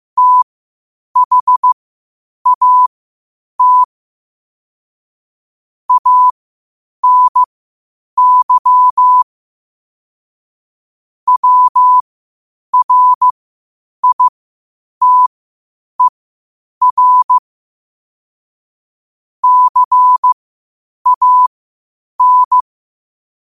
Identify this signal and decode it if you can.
Morse code, heard as a keyed tone, about 15 wpm, spaced as for 8 wpm, 1000 Hz. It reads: THAT ANY WRITER CAN
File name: qotdfiles / 8wpm